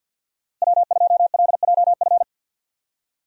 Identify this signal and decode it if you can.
Morse code, heard as a keyed tone, about 51 wpm, 700 Hz. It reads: W2LJF
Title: USA Call Signs - 5 Characters - Repeated 2x - 50wpm